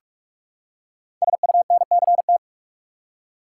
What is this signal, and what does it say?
Morse code, heard as a keyed tone, about 44 wpm, 700 Hz. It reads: SUNCT
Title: Code Groups - 5 Letter:Numbers - 45wpm